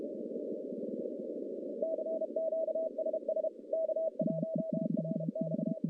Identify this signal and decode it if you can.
Morse code, heard as a keyed tone, about 33 wpm, 660 Hz, with noise. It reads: CQ SS K1RX
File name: contests / CW-SS-k5tr- -1200